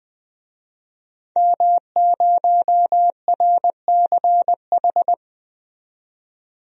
Morse code, heard as a keyed tone, about 20 wpm, 700 Hz. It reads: M0RCH
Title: DXCC-100 Call Signs - 5 Characters - 20wpm